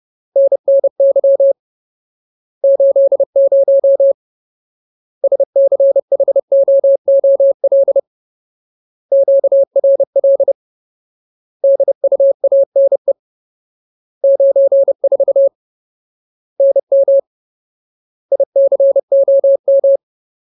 Morse code, heard as a keyed tone, about 30 wpm, 550 Hz. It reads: NNY 80 SCHOOL QRL DUANE 94 NM ICOM
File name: Ultimate Word List / 30WPM Ultimate